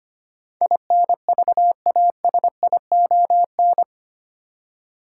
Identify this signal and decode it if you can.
Morse code, heard as a keyed tone, about 25 wpm, 700 Hz. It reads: INVASION
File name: 8 Letter Words - First 3 Letters Given - 25wpm